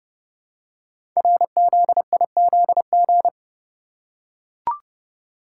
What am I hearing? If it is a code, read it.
Morse code, heard as a keyed tone, about 29 wpm, 700 Hz. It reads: RZIZG
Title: Code Groups - 5 Letters - 30wpm